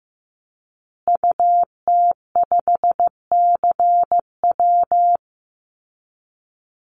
Morse code, heard as a keyed tone, about 15 wpm, 700 Hz. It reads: UT5CW
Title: Call Signs Repeated 2x - 15wpm